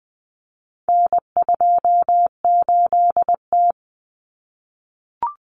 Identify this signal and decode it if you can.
Morse code, heard as a keyed tone, about 20 wpm, 700 Hz. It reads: N28T
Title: Common Cut Numbers - 4-Digit 20wpm